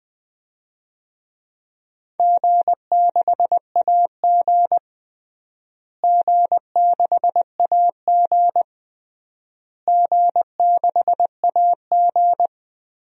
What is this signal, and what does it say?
Morse code, heard as a keyed tone, about 20 wpm, 700 Hz. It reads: G6AG G6AG G6AG
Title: DXCC-100 Call Signs - 4 Characters - Repeated 3x - 20wpm